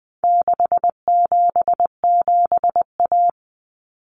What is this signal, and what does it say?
Morse code, heard as a keyed tone, about 20 wpm, 700 Hz. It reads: 677A
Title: Common Cut Numbers - 4-Digit 20wpm